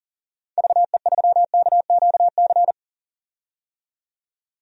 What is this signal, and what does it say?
Morse code, heard as a keyed tone, about 40 wpm, 700 Hz. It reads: VE3KQC
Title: DXCC-100 Call Signs - 6 Characters - 40wpm